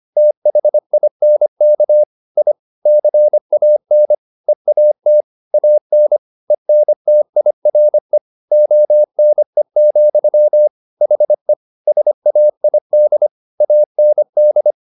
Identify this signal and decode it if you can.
Morse code, heard as a keyed tone, about 25 wpm, 600 Hz. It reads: THINK I CAN EAT AN ENTIRE ONE, HE SAID AND
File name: the old man and the sea part 1 25wpm